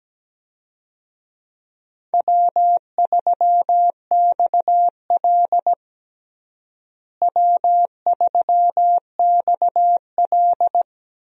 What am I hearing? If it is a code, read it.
Morse code, heard as a keyed tone, about 17 wpm, 700 Hz. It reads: W3XL W3XL
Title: USA Call Signs - 4 Characters - Repeated 3x - 17wpm